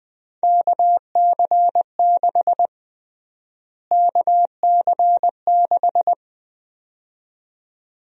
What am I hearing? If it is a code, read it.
Morse code, heard as a keyed tone, about 20 wpm, 700 Hz. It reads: KC6 KC6
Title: USA Call Sign Prefixes - Repeated 2x - 20wpm